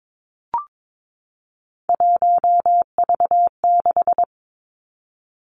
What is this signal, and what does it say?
Morse code, heard as a keyed tone, about 22 wpm, 700 Hz. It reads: 1V6
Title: Three Challenging Letter-Number Combinations 22wpm